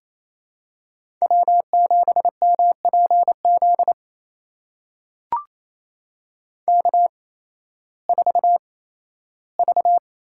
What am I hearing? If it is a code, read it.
Morse code, heard as a keyed tone, about 28 wpm, spaced as for 15 wpm, 700 Hz. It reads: W7MPZ K4V
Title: USA Call Signs - 5 Characters - 8x Character Spacing - 28wpm